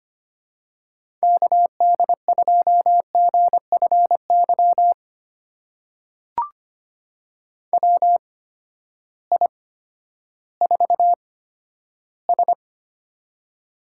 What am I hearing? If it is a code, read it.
Morse code, heard as a keyed tone, about 25 wpm, spaced as for 12 wpm, 700 Hz. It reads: KD2GFY WI4S
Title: USA Call Signs - 8x Character Spacing - 25wpm